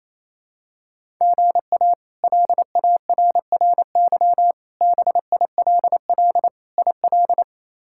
Morse code, heard as a keyed tone, about 28 wpm, 700 Hz. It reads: GA LARRY BILL IL
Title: K1USN SST QSO Practice - 28wpm